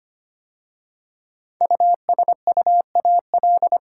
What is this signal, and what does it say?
Morse code, heard as a keyed tone, about 25 wpm, 700 Hz. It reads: USUAL
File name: 5 Letter Words - First 3 Letters Given - 25wpm